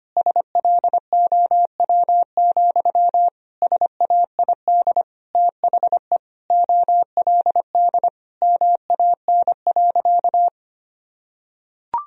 Morse code, heard as a keyed tone, about 25 wpm, 700 Hz. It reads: SLOW, SAID THE OLD MAN.
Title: Sentences from Top 1000 Words - Review Part 2 - 25wpm